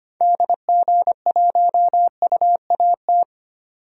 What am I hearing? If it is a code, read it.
Morse code, heard as a keyed tone, about 25 wpm, 700 Hz. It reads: DG1UAT
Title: DXCC-100 Call Signs - 6 Characters - 25wpm